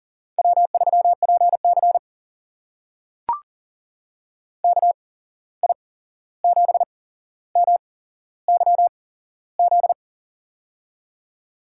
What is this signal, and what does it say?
Morse code, heard as a keyed tone, about 38 wpm, spaced as for 15 wpm, 700 Hz. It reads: W3PC KI7MYZ
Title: USA Call Signs - 8x Character Spacing - 40wpm